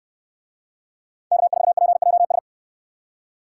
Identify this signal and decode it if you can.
Morse code, heard as a keyed tone, about 68 wpm, 700 Hz. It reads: K4PJS